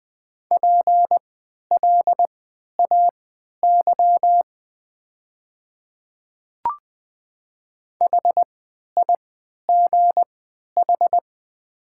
Morse code, heard as a keyed tone, about 20 wpm, spaced as for 11 wpm, 700 Hz. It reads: PLAY HIGH
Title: 3x Character Spacing - Top 200 Words - 20wpm